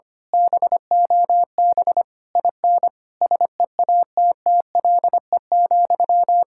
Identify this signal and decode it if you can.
Morse code, heard as a keyed tone, about 24 wpm, 700 Hz. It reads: BOB IN SEATTLE,
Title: Speed-Racing Common QSO Phrases - 25wpm